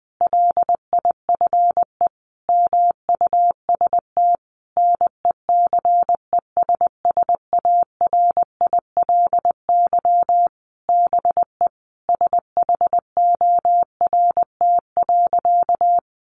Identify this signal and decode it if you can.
Morse code, heard as a keyed tone, about 20 wpm, 700 Hz. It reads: LIFE MUST NECESSARILY BE SHORT.
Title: Sentences from Top 1400 Words 20wpm